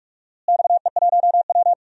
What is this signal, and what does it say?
Morse code, heard as a keyed tone, about 46 wpm, 700 Hz. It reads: XE1W
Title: DXCC-100 Call Signs - 4 Characters - Repeated 2x - 45wpm